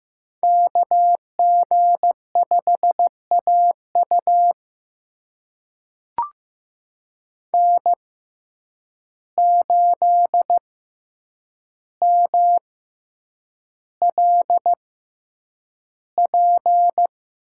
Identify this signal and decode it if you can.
Morse code, heard as a keyed tone, about 15 wpm, spaced as for 8 wpm, 700 Hz. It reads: KG5AU N8MLP
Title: USA Call Signs - 5 Characters - 6x Character Spacing - 15wpm